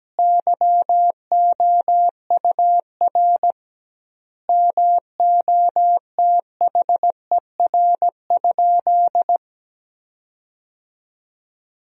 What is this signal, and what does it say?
Morse code, heard as a keyed tone, about 17 wpm, 700 Hz. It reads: YOUR MOTHER?
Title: 2x Word-Spacing - Sentences from Top 200 Words - 17wpm